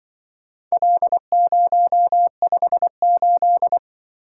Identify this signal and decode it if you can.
Morse code, heard as a keyed tone, about 24 wpm, 700 Hz. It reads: L058
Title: Speed-Racing - Four Letter-Number Combinations - 20wpm